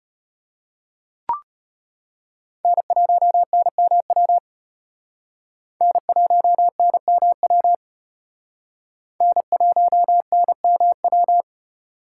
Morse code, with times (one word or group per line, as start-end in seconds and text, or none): 2.6-4.4 N1NMW
5.8-7.7 N1NMW
9.2-11.4 N1NMW